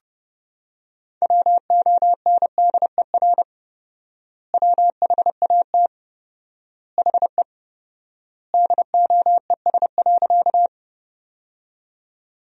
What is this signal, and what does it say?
Morse code, heard as a keyed tone, about 30 wpm, 700 Hz. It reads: WONDER WHAT HE DOES.
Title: 4x Word-Spacing - Sentences from Top 500 Words - 30wpm